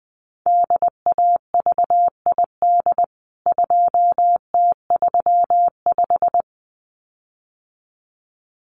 Morse code, heard as a keyed tone, about 19 wpm, 700 Hz. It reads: DAVID 2T35
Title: CWT Exchange 20wpm